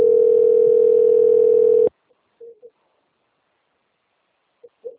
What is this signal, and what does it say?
Ringback tone (North America), 440 + 480 Hz.